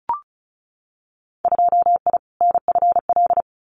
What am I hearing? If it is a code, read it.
Morse code, heard as a keyed tone, about 36 wpm, 700 Hz. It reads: 2I NFL
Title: Winter Field Day Exchanges 35wpm